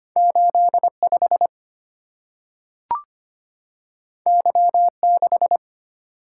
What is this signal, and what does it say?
Morse code, heard as a keyed tone, about 25 wpm, 700 Hz. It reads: 85 Y6
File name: Two Letter-Number Combinations - 25wpm